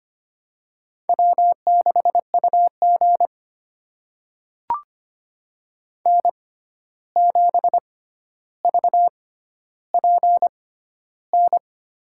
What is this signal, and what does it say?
Morse code, heard as a keyed tone, about 25 wpm, spaced as for 12 wpm, 700 Hz. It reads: W6UG N7VPN
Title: USA Call Signs - 6x Character Spacing - 25wpm